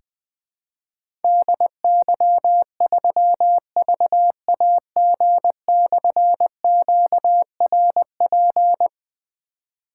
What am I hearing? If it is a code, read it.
Morse code, heard as a keyed tone, about 20 wpm, 700 Hz. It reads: DY3VAG/QRP
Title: Difficult Call Signs 20wpm